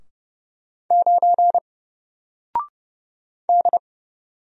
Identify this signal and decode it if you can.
Morse code, heard as a keyed tone, about 30 wpm, 700 Hz. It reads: 9 D
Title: Single Character (Letter:Number:Punctuation:Prosign) Alt 30wpm